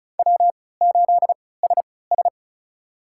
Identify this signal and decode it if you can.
Morse code, heard as a keyed tone, about 36 wpm, spaced as for 20 wpm, 700 Hz. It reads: W8SS
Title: USA Call Signs - 3x Character Spacing - 35wpm